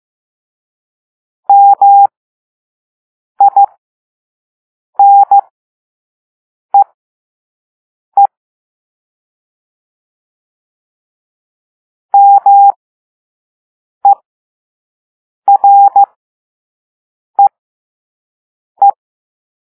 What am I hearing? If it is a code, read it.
Morse code, heard as a keyed tone, about 15 wpm, spaced as for 5 wpm, 800 Hz. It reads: MINEE MEREE